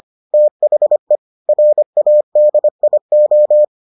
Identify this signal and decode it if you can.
Morse code, heard as a keyed tone, about 25 wpm, 600 Hz. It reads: THE RADIO